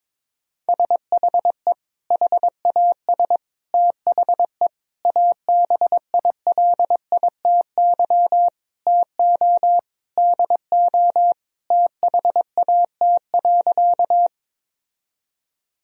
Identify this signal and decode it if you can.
Morse code, heard as a keyed tone, about 22 wpm, 700 Hz. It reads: SHE HAS THE ABILITY TO DO THAT.
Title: Sentences from Top 1000 Words - Review Part 3 - 22wpm